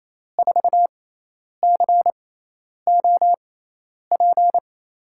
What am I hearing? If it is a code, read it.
Morse code, heard as a keyed tone, about 28 wpm, spaced as for 10 wpm, 700 Hz. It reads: 4COP
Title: USA Call Signs - 5 Characters - 6x Character Spacing - 28wpm